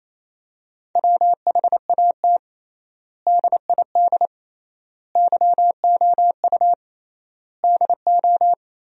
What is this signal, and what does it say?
Morse code, heard as a keyed tone, about 28 wpm, 700 Hz. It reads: WHAT DID YOU DO
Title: 3x Word-Spacing - Sentences from Top 100 Words - 28wpm